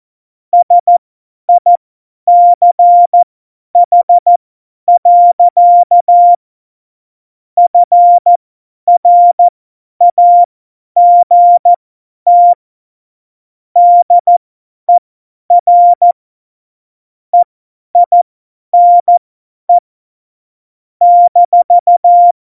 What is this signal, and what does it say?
Morse code, heard as a keyed tone, about 14 wpm, spaced as for 10 wpm, 700 Hz. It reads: SICH. FRAGT DER EINE -
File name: Morsekurs / Witze / Witz 31.Juli 2018, 50 F70